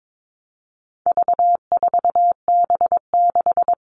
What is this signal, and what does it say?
Morse code, heard as a keyed tone, about 22 wpm, 700 Hz. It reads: V4B6
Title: All Cut Numbers - 4-Digit 22wpm